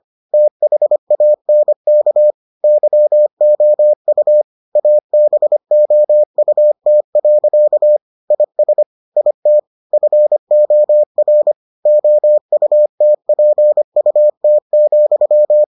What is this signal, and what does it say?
Morse code, heard as a keyed tone, about 25 wpm, 600 Hz. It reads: THANK YOU ABOUT. IS IT FOR OUTPUT,